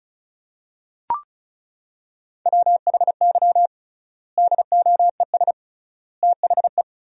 Morse code, heard as a keyed tone, about 36 wpm, 700 Hz. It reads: WHY DOES THE
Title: 3x Word-Spacing - Sentences from Top 500 Words - 35wpm